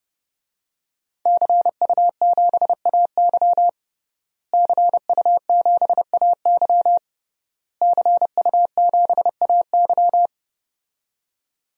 Morse code, heard as a keyed tone, about 30 wpm, 700 Hz. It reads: CU7AY CU7AY CU7AY
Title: DXCC-100 Call Signs - 5 Characters - Repeated 3x - 30wpm